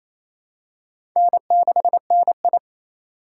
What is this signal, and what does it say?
Morse code, heard as a keyed tone, about 28 wpm, 700 Hz. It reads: N6NI
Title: USA Call Signs - 4x Character Spacing - 28wpm